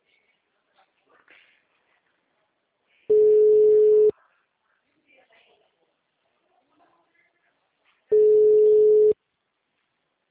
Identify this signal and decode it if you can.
Ringback tone (Europe, 425 Hz), with speech.